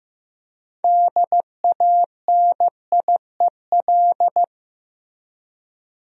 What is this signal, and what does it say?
Morse code, heard as a keyed tone, about 15 wpm, 700 Hz. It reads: DANIEL
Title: USA Common Names - 15wpm